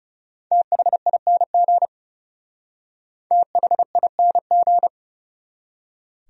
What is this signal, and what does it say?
Morse code, heard as a keyed tone, about 36 wpm, 700 Hz. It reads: THING THING
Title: Speed-Racing Top 300 Words 30wpm